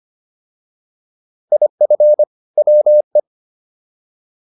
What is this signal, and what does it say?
Morse code, heard as a keyed tone, about 25 wpm, 600 Hz. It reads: IF WE